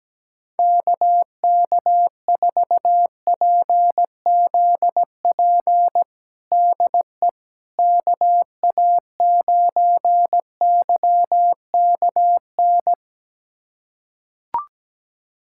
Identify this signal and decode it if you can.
Morse code, heard as a keyed tone, about 17 wpm, 700 Hz. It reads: KK4PZP DE KA9YKN
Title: USA Callsign DE Callsign - 6 Characters - 17wpm